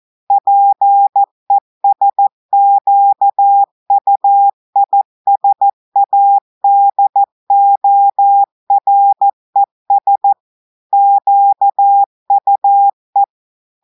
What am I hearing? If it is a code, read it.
Morse code, heard as a keyed tone, about 14 wpm, 800 Hz. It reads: PESQUISADORES QUE
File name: audios / portugais / portugais texte en clair 1 vitesse 14 mots minutes